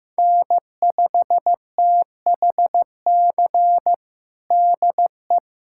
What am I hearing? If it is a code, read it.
Morse code, heard as a keyed tone, about 15 wpm, 700 Hz. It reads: N5THC DE
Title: USA Callsign DE Callsign - 5 Characters - 15wpm